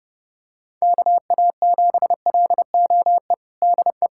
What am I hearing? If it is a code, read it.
Morse code, heard as a keyed tone, about 31 wpm, 700 Hz. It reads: KA7LOE DE
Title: USA Callsign DE Callsign - 6 Characters - 30wpm